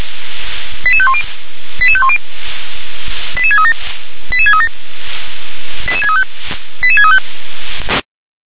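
Five-tone selective call (ZVEI), heard as quick runs of five tones, tones of about 70 ms. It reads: 88510 88510 88637 88637 88635 88635
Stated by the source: ZVEI five-tone selective call 88511, received on SDR.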